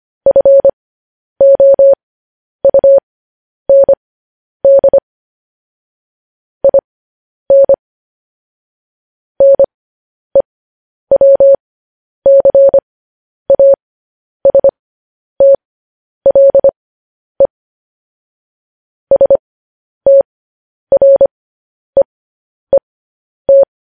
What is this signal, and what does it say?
Morse code, heard as a keyed tone, about 25 wpm, spaced as for 10 wpm, 550 Hz. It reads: FOUND IN NEWCASTLE STREET
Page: news-headlines-in-morse-code-at-10-wpm